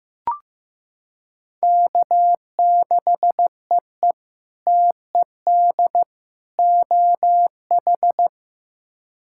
Morse code, heard as a keyed tone, about 16 wpm, 700 Hz. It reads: K6EE TED OH
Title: K1USN SST QSO Practice - 15wpm